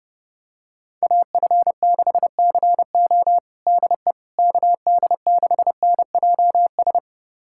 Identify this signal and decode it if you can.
Morse code, heard as a keyed tone, about 32 wpm, 700 Hz. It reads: AF6CO DE KD6NJS
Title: USA Callsign DE Callsign 30wpm